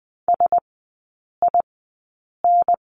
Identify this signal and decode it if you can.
Morse code, heard as a keyed tone, about 19 wpm, spaced as for 8 wpm, 700 Hz. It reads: SIN
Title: 172 Lesson 28e - Number 9 - Three Characters 20wpm